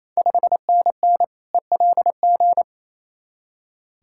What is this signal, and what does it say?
Morse code, heard as a keyed tone, about 27 wpm, 690 Hz. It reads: 5NN ELG
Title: Ontario QSO Party Exchange 28wpm